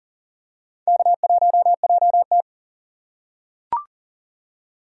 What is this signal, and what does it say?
Morse code, heard as a keyed tone, about 40 wpm, 700 Hz. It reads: K1JT
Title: USA Call Signs - 4 Characters - 4x Character Spacing - 40wpm